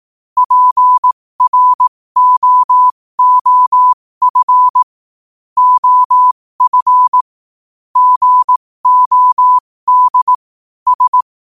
Morse code, heard as a keyed tone, about 18 wpm, spaced as for 15 wpm, 1000 Hz. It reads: PROOF OF GOD S
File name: qotdfiles / 15wpm